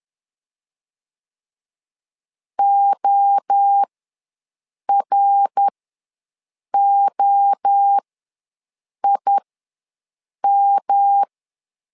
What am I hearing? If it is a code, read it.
Morse code, heard as a keyed tone, about 10 wpm, spaced as for 6 wpm, 800 Hz. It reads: OROIM
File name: LEZIONI / 3 MWRO / 3 MWRO ws 10wpm 2310